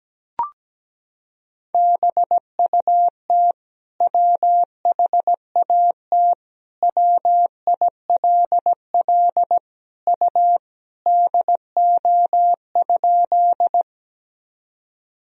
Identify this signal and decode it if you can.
Morse code, heard as a keyed tone, about 17 wpm, 700 Hz. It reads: BUT WHAT WILL U DO?
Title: Sentences from Top 100 Words and Abbreviations 17wpm